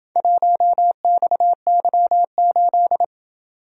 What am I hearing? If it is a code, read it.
Morse code, heard as a keyed tone, about 27 wpm, 700 Hz. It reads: 1XY8